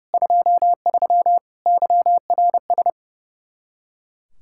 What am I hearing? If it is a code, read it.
Morse code, heard as a keyed tone, about 36 wpm, 700 Hz. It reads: 23 YRS
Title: Speed-Racing Common QSO Phrases - 30wpm